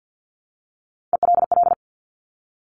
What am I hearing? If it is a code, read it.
Morse code, heard as a keyed tone, about 50 wpm, 700 Hz. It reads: ELL
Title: Top Three Letter Combinations in Words 50wpm